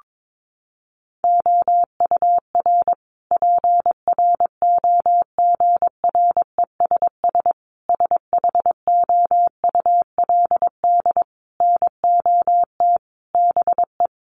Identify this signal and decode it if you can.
Morse code, heard as a keyed tone, about 22 wpm, 700 Hz. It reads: OUR PROGRESS SHOULD NOT BE
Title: Sentences from Top 1400 Words 22wpm